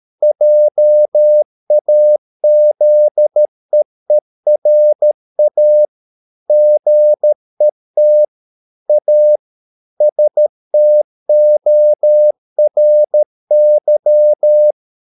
Morse code, heard as a keyed tone, about 13 wpm, 600 Hz. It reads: JAZEERA GET A STORY